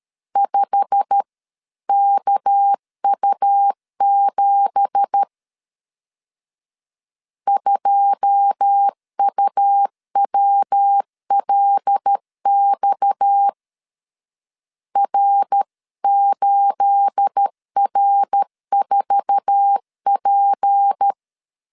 Morse code, heard as a keyed tone, about 13 wpm, 800 Hz. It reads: 5KU7 2UWLX R8R4P
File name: CW / DET / 1 9 1500C S B 12wpm 3819